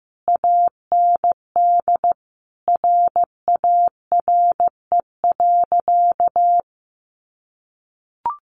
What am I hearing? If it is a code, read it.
Morse code, heard as a keyed tone, about 15 wpm, 700 Hz. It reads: AND RARE.
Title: Sentences from Top 1100 Words 15wpm